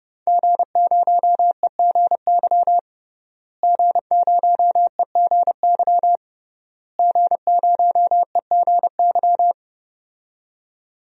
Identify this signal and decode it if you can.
Morse code, heard as a keyed tone, about 30 wpm, 700 Hz. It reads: G0EGY G0EGY G0EGY
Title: DXCC-100 Call Signs - 5 Characters - Repeated 3x - 30wpm